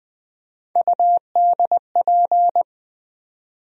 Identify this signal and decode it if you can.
Morse code, heard as a keyed tone, about 20 wpm, 700 Hz. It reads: UDP
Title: USA Call Sign Postfixes - 3 Letters 20wpm